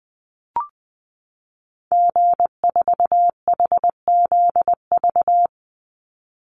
Morse code, heard as a keyed tone, about 19 wpm, 700 Hz. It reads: G4HZV
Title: Call Signs Repeated 3x - 20wpm